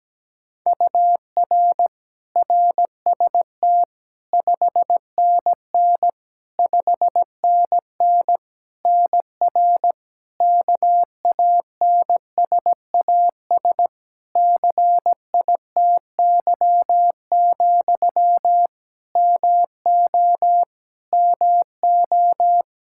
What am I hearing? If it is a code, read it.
Morse code, heard as a keyed tone, about 17 wpm, 700 Hz. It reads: UR RST 5NN 5NN NR KANSAS CITY, MO MO
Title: Practice QSO - Exchange 1 - Beginning - 17wpm